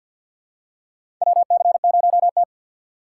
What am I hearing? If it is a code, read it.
Morse code, heard as a keyed tone, about 53 wpm, 700 Hz. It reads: WC0T
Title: USA Call Signs - 4 Characters - 3x Character Spacing - 50wpm